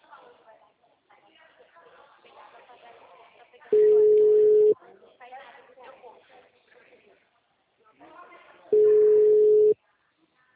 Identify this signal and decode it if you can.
Ringback tone (Europe, 425 Hz), with speech.